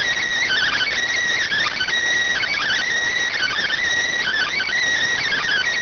Emergency Alert System (SAME) tones (header bursts), heard as harsh buzzing bursts. It is heard from 0.3 to 4.8 s.